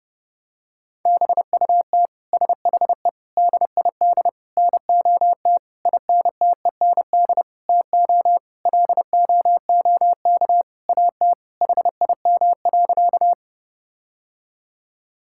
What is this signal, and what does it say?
Morse code, heard as a keyed tone, about 30 wpm, 700 Hz. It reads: BUT SHE DID NOT INTEND TO LOOK AT HIM.
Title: Sentences from Top 1600 Words 30wpm